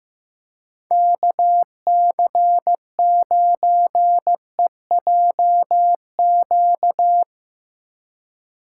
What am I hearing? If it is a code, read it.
Morse code, heard as a keyed tone, about 16 wpm, 700 Hz. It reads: KC9EJQ
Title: Speed Racing - USA Call Signs 15wpm